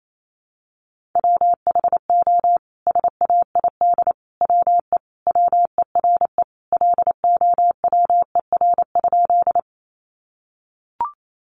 Morse code, heard as a keyed tone, about 28 wpm, 700 Hz. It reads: WHO SAID WE WERE LOWER?
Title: Sentences from Top 700 Words 28wpm